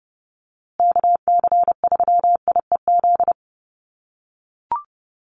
Morse code, heard as a keyed tone, about 30 wpm, 700 Hz. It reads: KC3IEZ
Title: USA Call Signs Repeated 3x 30wpm